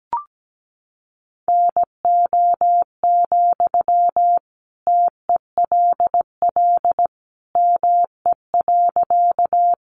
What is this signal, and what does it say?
Morse code, heard as a keyed tone, about 17 wpm, 700 Hz. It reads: NO, TELL ME.
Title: Sentences from Top 200 Words 17wpm